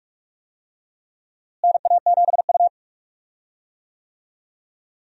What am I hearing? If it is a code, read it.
Morse code, heard as a keyed tone, about 44 wpm, 700 Hz. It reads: NA7U
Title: USA Call Signs - 4 Characters - Repeated 2x - 45wpm